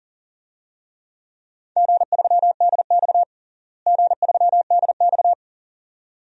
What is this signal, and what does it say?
Morse code, heard as a keyed tone, about 40 wpm, 690 Hz. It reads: G3DX G3DX
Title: DXCC-100 Call Signs - 4 Characters - Repeated 2x - 40wpm